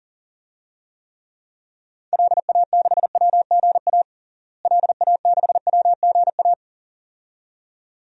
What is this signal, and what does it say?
Morse code, heard as a keyed tone, about 37 wpm, 700 Hz. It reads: LA6WGA LA6WGA
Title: DXCC-100 Call Signs - 6 Characters - Repeated 2x - 40wpm